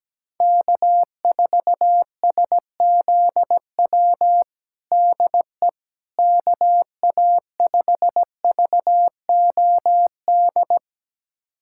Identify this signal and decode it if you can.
Morse code, heard as a keyed tone, about 17 wpm, 700 Hz. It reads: K4SZW DE KA5VOD
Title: USA Callsign DE Callsign 17wpm